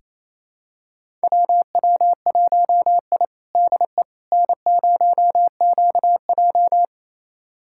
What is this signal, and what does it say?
Morse code, heard as a keyed tone, about 28 wpm, 700 Hz. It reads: WW1I DE N0QJ
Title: USA Callsign DE Callsign - 4 Characters - 28wpm